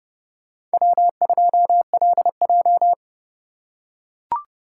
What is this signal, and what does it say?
Morse code, heard as a keyed tone, about 29 wpm, 700 Hz. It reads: W2LJ
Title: USA Call Signs - 4 Characters - 2x Character Spacing - 30wpm